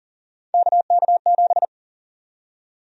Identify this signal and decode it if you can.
Morse code, heard as a keyed tone, about 40 wpm, 700 Hz. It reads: KK7
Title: USA Call Sign Prefixes - 40wpm